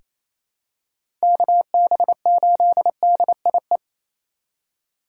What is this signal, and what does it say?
Morse code, heard as a keyed tone, about 27 wpm, 700 Hz. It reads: KB8DIE
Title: USA Call Signs - 6 Characters - 4x Character Spacing - 28wpm